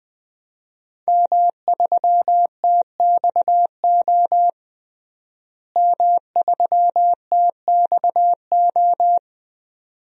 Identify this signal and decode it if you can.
Morse code, heard as a keyed tone, about 20 wpm, 700 Hz. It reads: M3TXO M3TXO
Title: DXCC-100 Call Signs - 5 Characters - Repeated 2x - 20wpm